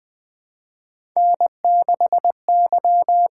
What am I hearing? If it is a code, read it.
Morse code, heard as a keyed tone, about 20 wpm, 700 Hz. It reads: N6Y
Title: USA Call Signs - 8x Character Spacing - 20wpm